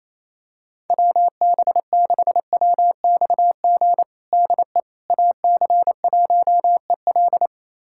Morse code, heard as a keyed tone, about 28 wpm, 700 Hz. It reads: WB6WXG DE AC1EL
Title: USA Callsign DE Callsign 28wpm